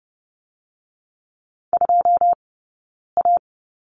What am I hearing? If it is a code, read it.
Morse code, heard as a keyed tone, about 30 wpm, 700 Hz. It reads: 2 A
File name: 092 Lesson 16d - Letter c - Two Characters 30wpm